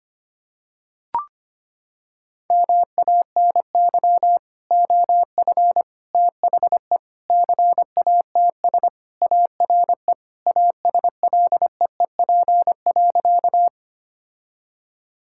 Morse code, heard as a keyed tone, about 25 wpm, 700 Hz. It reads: MANY OF THE CATS ARE ASLEEP.
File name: practice / 25wpm / Sentences with 3-Word Phrases - 25wpm